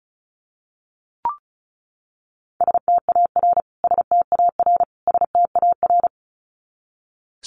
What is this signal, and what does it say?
Morse code, heard as a keyed tone, about 34 wpm, 700 Hz. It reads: STAR STAR STAR
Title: Top 500 Words Repeated 35wpm